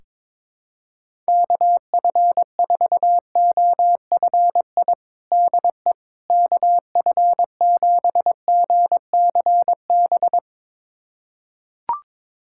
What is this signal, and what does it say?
Morse code, heard as a keyed tone, about 22 wpm, 700 Hz. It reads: KF4OFI DE KF7GCB
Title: USA Callsign DE Callsign - 6 Characters - 22wpm